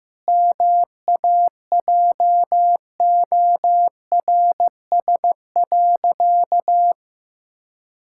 Morse code, heard as a keyed tone, about 15 wpm, 700 Hz. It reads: MAJORS.
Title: Pangram Challenge v2 - 15wpm